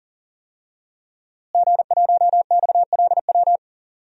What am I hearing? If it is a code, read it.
Morse code, heard as a keyed tone, about 40 wpm, 700 Hz. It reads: G1XLW
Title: DXCC-100 Call Signs - 5 Characters - Repeated 2x - 40wpm